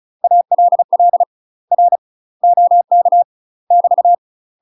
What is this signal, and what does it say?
Morse code, heard as a keyed tone, about 34 wpm, 700 Hz. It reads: ALL R OK <BT>
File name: UK MASTER QSOs TRAINING / 35WPM / UK MASTER QSOs 0025